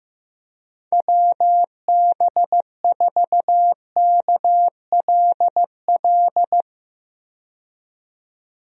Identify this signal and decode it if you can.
Morse code, heard as a keyed tone, about 15 wpm, 700 Hz. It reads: WB4KLL
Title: USA Call Signs - 6 Characters - 15wpm